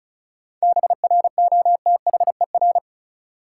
Morse code, heard as a keyed tone, about 36 wpm, 700 Hz. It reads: BROTHER
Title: Seven Letter Words - 35wpm